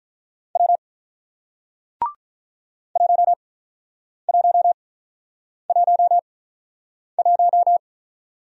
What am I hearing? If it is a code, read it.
Morse code, heard as a keyed tone, about 48 wpm, spaced as for 9 wpm, 700 Hz. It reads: W 1111